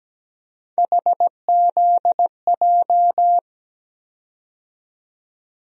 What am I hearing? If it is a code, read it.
Morse code, heard as a keyed tone, about 17 wpm, 700 Hz. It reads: HZJ
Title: USA Call Sign Postfixes - 3 Letters 17wpm